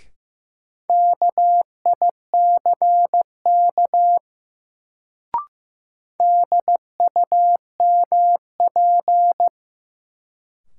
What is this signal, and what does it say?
Morse code, heard as a keyed tone, about 15 wpm, 700 Hz. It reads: KICK DUMP
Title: Sentences from Top 1300 Words - New Word Drill - 15wpm